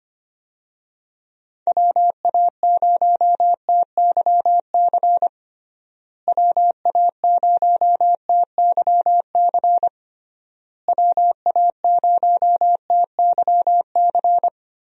Morse code, heard as a keyed tone, about 25 wpm, 700 Hz. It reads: WA0TYC WA0TYC WA0TYC
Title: USA Call Signs - 6 Characters - Repeated 3x - 25wpm